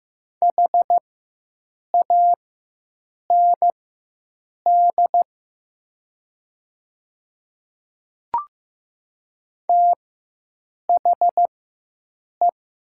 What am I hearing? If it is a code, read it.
Morse code, heard as a keyed tone, about 15 wpm, spaced as for 7 wpm, 700 Hz. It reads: HAND THE
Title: 4x Character Spacing - Top 300 Words - 15wpm